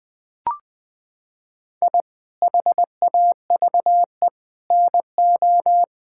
Morse code, heard as a keyed tone, about 20 wpm, 700 Hz. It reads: I HAVE NO
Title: Sentences from Top 1000 Words - Review Part 6 - 20wpm